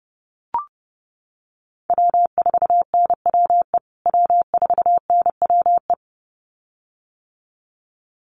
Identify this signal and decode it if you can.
Morse code, heard as a keyed tone, about 30 wpm, 700 Hz. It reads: W4NWE W4NWE
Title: USA Call Signs Repeated 2x 30wpm